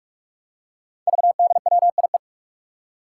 Morse code, heard as a keyed tone, about 44 wpm, 700 Hz. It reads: VDWIE
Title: Five Challenging Letter-Number Combinations 45wpm